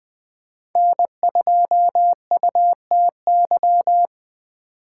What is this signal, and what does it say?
Morse code, heard as a keyed tone, about 20 wpm, 700 Hz. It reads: N2UTY